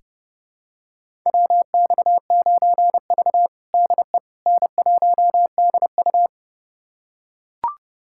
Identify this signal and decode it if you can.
Morse code, heard as a keyed tone, about 32 wpm, 700 Hz. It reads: WX9V DE N1DU
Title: USA Callsign DE Callsign - 4 Characters - 30wpm